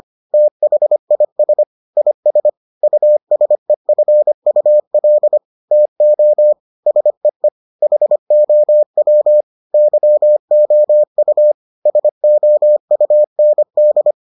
Morse code, heard as a keyed tone, about 25 wpm, 600 Hz. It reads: THIS IS USEFUL TO SEE HOW YOU SOUND